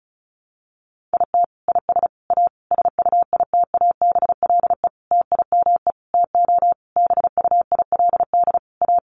Morse code, heard as a keyed tone, about 34 wpm, 700 Hz. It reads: IT IS A SUITABLE TIME TO BUILD A
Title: Sentences from Top 1400 Words 35wpm